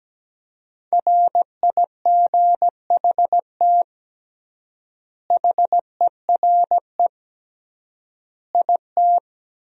Morse code, heard as a keyed tone, about 17 wpm, 700 Hz. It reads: RIGHT HERE IT
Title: 3x Word-Spacing - Sentences from Top 300 Words - 17wpm